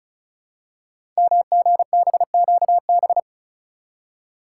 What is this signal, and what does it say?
Morse code, heard as a keyed tone, about 36 wpm, 700 Hz. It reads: MGBQB
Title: Code Groups - 5 Letters - 35wpm